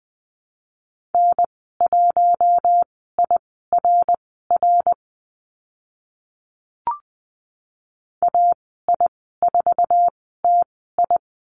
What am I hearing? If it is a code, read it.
Morse code, heard as a keyed tone, about 20 wpm, spaced as for 15 wpm, 700 Hz. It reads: N1IRR AI4TI
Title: 055 Lesson 10g - Letter r - Call Signs 20wpm